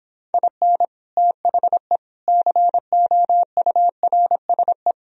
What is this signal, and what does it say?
Morse code, heard as a keyed tone, about 26 wpm, 700 Hz. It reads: IN THE COURSE